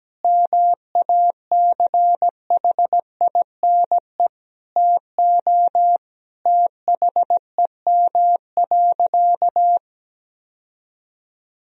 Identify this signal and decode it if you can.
Morse code, heard as a keyed tone, about 17 wpm, 700 Hz. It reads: MACHINE TO THEM.
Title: Sentences from Top 1000 Words - Review Part 2 - 17wpm